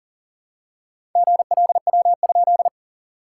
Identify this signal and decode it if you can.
Morse code, heard as a keyed tone, about 40 wpm, 660 Hz. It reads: GLW?
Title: SOTA QSO Practice 40wpm